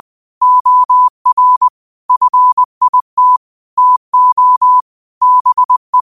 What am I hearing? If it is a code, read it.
Morse code, heard as a keyed tone, about 20 wpm, 1000 Hz. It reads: OR FIT TO BE
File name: qotdfiles / 20wpm